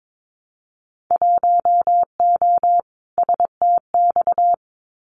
Morse code, heard as a keyed tone, about 22 wpm, 700 Hz. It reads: 1O STX
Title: Winter Field Day Exchanges 22wpm